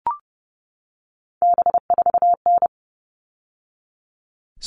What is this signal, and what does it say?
Morse code, heard as a keyed tone, about 30 wpm, 700 Hz. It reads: B4N
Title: All Cut Numbers - 3-Digit 30wpm